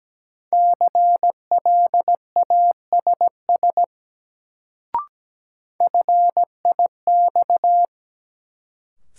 Morse code, heard as a keyed tone, about 17 wpm, 700 Hz. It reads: CLASS FIX
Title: Sentences from Top 1400 Words - New Word Drill Alt - 17wpm